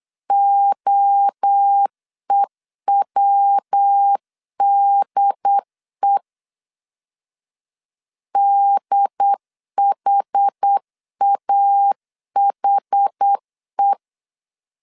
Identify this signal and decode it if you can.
Morse code, heard as a keyed tone, about 8 wpm, 800 Hz. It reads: OEWDE DHAHE